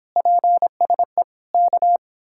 Morse code, heard as a keyed tone, about 29 wpm, 700 Hz. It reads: PSE K